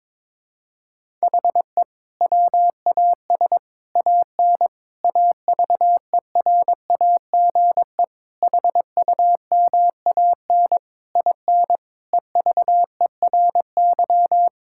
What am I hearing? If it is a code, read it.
Morse code, heard as a keyed tone, about 22 wpm, 700 Hz. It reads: HE WAS AN AVERAGE HUMAN IN EVERY
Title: Sentences from Top 1000 Words - Review Part 6 - 22wpm